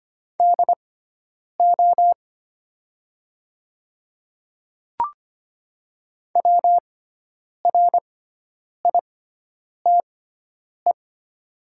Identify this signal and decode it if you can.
Morse code, heard as a keyed tone, about 25 wpm, spaced as for 9 wpm, 700 Hz. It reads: DO WRITE
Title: 6x Character Spacing - Top 100 Words - 25wpm